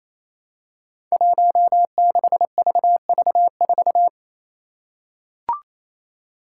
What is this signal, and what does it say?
Morse code, heard as a keyed tone, about 28 wpm, 700 Hz. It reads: 16VV4
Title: Five Challenging Letter-Number Combinations 28wpm